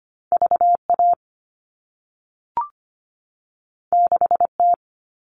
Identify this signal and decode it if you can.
Morse code, heard as a keyed tone, about 25 wpm, 700 Hz. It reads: VA 6T
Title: All Cut Numbers - 2-Digit 25wpm